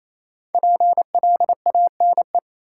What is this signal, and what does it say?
Morse code, heard as a keyed tone, about 28 wpm, 700 Hz. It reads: PLANE
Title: Sentences from Top 1000 Words - Review Part 3 - New Word Drill - 28wpm